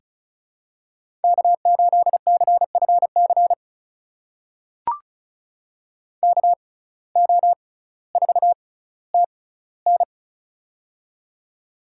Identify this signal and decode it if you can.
Morse code, heard as a keyed tone, about 36 wpm, spaced as for 18 wpm, 700 Hz. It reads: K8CFC KO4TN
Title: USA Call Signs - 5 Characters - 6x Character Spacing - 35wpm